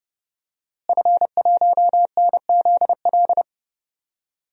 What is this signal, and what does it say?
Morse code, heard as a keyed tone, about 32 wpm, 700 Hz. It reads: F1NZL
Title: DXCC-100 Call Signs - 5 Characters - Repeated 3x - 30wpm